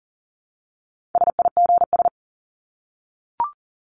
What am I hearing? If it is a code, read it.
Morse code, heard as a keyed tone, about 40 wpm, 700 Hz. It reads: SIGS
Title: Abbreviations Repeated 3x - 40wpm